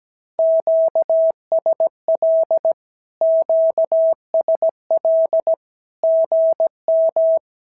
Morse code, heard as a keyed tone, about 17 wpm, 650 Hz. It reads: QSL QSL GM
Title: SOTA QSO Practice 17wpm